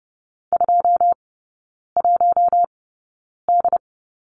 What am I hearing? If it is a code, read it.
Morse code, heard as a keyed tone, about 30 wpm, spaced as for 9 wpm, 700 Hz. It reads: 21D
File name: Lesson 18e - Letter m - Three Characters 30wpm